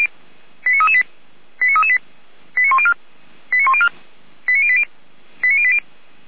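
Five-tone selective call (ZVEI), heard as quick runs of five tones, tones of about 70 ms. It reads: 89338 89338 89104 89104 89080 89080
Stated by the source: ZVEI five-tone selective call 89126 received on an SDR.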